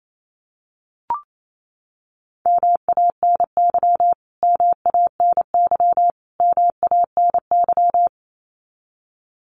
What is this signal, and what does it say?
Morse code, heard as a keyed tone, about 28 wpm, 700 Hz. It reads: MANY MANY MANY
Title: Top 500 Words Repeated 28wpm